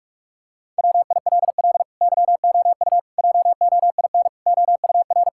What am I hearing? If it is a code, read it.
Morse code, heard as a keyed tone, about 42 wpm, 700 Hz. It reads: WILL YOU JOIN OUR
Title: Sentences from Top 1600 Words 45wpm